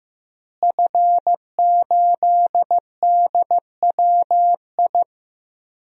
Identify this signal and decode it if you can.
Morse code, heard as a keyed tone, about 15 wpm, 700 Hz. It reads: F8DWI
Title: DXCC-100 Call Signs - 5 Characters - Repeated 3x - 15wpm